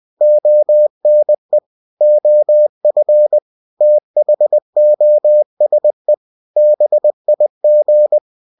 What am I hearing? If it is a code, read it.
Morse code, heard as a keyed tone, about 20 wpm, 600 Hz. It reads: ONE OF THOSE BIG